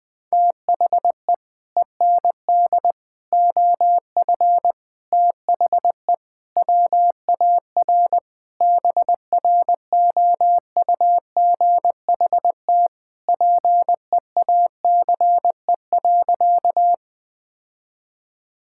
Morse code, heard as a keyed tone, about 21 wpm, 700 Hz. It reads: THE END OF THE WAR BROUGHT PEACE.
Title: Sentences with 5-Word Phrases - Spoken Beginning - 20wpm